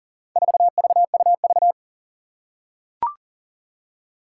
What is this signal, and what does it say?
Morse code, heard as a keyed tone, about 40 wpm, 700 Hz. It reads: 4VUV
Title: Four Challenging Letter-Number Combinations 40wpm